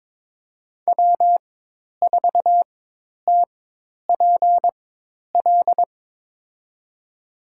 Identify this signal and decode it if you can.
Morse code, heard as a keyed tone, about 22 wpm, spaced as for 10 wpm, 700 Hz. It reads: W4TPL
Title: USA Call Signs - 5 Characters - 4x Character Spacing - 22wpm